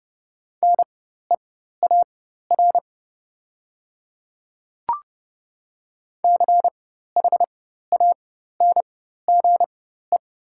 Morse code, heard as a keyed tone, about 30 wpm, spaced as for 14 wpm, 700 Hz. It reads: NEAR CHANGE
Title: 4x Character Spacing - Top 200 Words - 30wpm